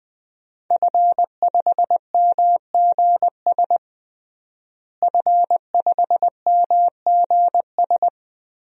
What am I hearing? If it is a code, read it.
Morse code, heard as a keyed tone, about 20 wpm, 700 Hz. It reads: F5MGS F5MGS
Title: DXCC-100 Call Signs - 5 Characters - Repeated 3x - 20wpm